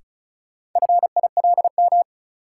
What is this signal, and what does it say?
Morse code, heard as a keyed tone, about 36 wpm, 700 Hz. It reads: FILM
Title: Sentences from Top 1000 Words - Review Part 3 - New Word Drill - 35wpm